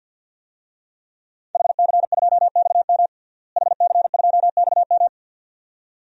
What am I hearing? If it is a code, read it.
Morse code, heard as a keyed tone, about 53 wpm, 700 Hz. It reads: HC2XM HC2XM
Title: DXCC-100 Call Signs - 5 Characters - Repeated 2x - 50wpm